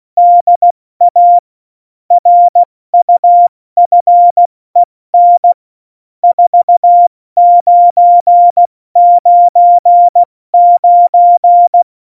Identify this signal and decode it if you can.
Morse code, heard as a keyed tone, about 16 wpm, spaced as for 14 wpm, 700 Hz. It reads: DA RUFEN 4999
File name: Morsekurs / Witze / Witz 03.Juli 201, 70 F80